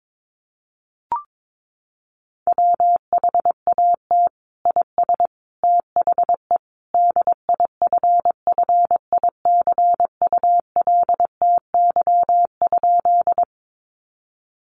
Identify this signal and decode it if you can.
Morse code, heard as a keyed tone, about 22 wpm, 700 Hz. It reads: WHAT IS THE DIFFICULTY?
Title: Sentences from Top 1500 Words 22wpm